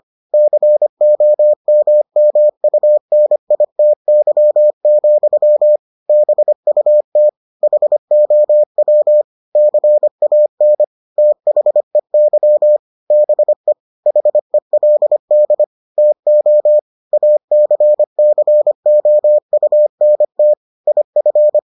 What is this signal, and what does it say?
Morse code, heard as a keyed tone, about 25 wpm, 600 Hz. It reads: COMMUNITY, BUT HOW CAN THEY BE HELD TO ACCOUNT IF